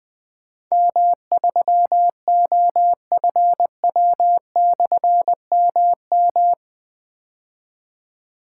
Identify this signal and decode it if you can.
Morse code, heard as a keyed tone, about 20 wpm, 700 Hz. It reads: M3OFW/MM
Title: Difficult Call Signs 20wpm